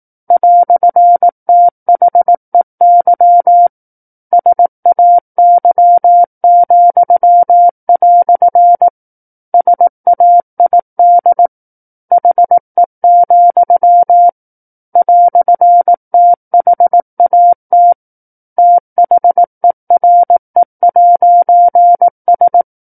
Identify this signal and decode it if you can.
Morse code, heard as a keyed tone, about 18 wpm, 700 Hz. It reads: "THEY SAY," SAID HE, "THAT THERE'S
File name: 18 WPM